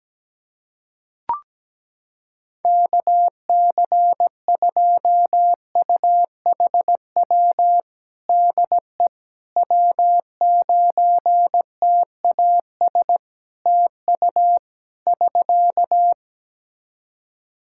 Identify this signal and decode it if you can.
Morse code, heard as a keyed tone, about 17 wpm, 700 Hz. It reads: KC2UHW DE W9TAS TU <SK>
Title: Practice QSO - Exchange 4 - Ending - 17wpm